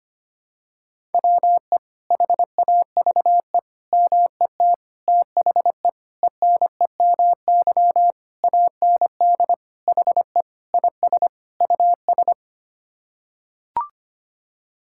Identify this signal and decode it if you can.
Morse code, heard as a keyed tone, about 25 wpm, 700 Hz. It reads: WE HAVE MET THE ENEMY AND HE IS US
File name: Sets of 9 Words - 25wpm